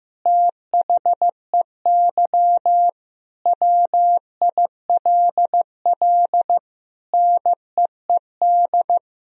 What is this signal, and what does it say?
Morse code, heard as a keyed tone, about 15 wpm, 700 Hz. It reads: THEY WILL NEED
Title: Sentences from Top 1000 Words - Review Part 4 - 15wpm